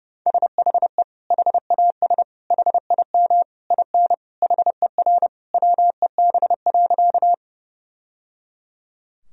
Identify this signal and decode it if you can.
Morse code, heard as a keyed tone, about 30 wpm, 700 Hz. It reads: SHE HAS HIM IN HER WEB.
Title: Sentences from Top 1000 Words - Review Part 4 - 30wpm